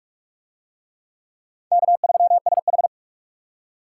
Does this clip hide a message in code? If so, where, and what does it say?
Yes: Morse code from 1.7 to 2.9 s, heard as a keyed tone, about 45 wpm, 700 Hz. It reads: K3SH